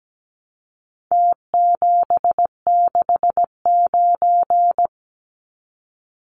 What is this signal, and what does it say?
Morse code, heard as a keyed tone, about 17 wpm, 700 Hz. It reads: T769
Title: Numbers - Contest Style - 4-Digit 17wpm